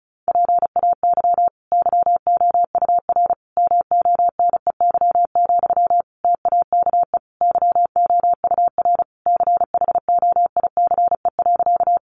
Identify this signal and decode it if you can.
Morse code, heard as a keyed tone, about 34 wpm, 700 Hz. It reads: PAY YOUR MONEY, TAKE YOUR CHOICE.
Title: Sentences from Top 900 Words 35wpm